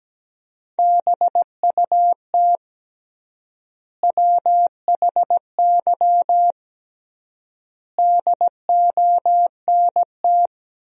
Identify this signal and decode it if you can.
Morse code, heard as a keyed tone, about 17 wpm, 700 Hz. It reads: BUT WHY DONT
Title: 3x Word-Spacing - Sentences from Top 300 Words - 17wpm